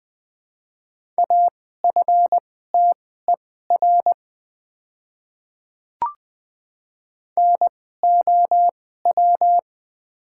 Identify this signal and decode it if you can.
Morse code, heard as a keyed tone, about 20 wpm, spaced as for 15 wpm, 700 Hz. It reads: AFTER NOW
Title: 2x Character Spacing - Top 200 Words - 20wpm